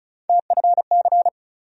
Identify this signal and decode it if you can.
Morse code, heard as a keyed tone, about 36 wpm, 700 Hz. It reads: TFC
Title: USA Call Sign Postfixes - 3 Letters 35wpm